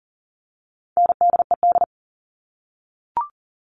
Morse code, heard as a keyed tone, about 40 wpm, 700 Hz. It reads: NDED
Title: All Cut Numbers - 4-Digit 40wpm